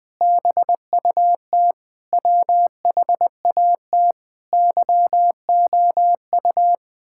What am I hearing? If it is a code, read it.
Morse code, heard as a keyed tone, about 20 wpm, 700 Hz. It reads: BUT WHAT YOU
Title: Sentences with 9-Word Phrases - Spoken Beginning - 20wpm